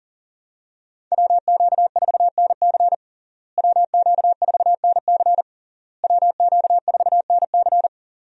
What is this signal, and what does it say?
Morse code, heard as a keyed tone, about 37 wpm, 700 Hz. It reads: WQ4NC WQ4NC WQ4NC
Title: USA Call Signs - 5 Characters - Repeated 3x - 40wpm